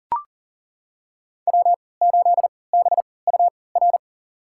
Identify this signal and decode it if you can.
Morse code, heard as a keyed tone, about 37 wpm, spaced as for 22 wpm, 700 Hz. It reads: W8BUR
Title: USA Call Signs - 5 Characters - 3x Character Spacing - 40wpm